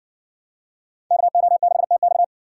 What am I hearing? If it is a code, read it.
Morse code, heard as a keyed tone, about 58 wpm, 700 Hz. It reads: KO6TX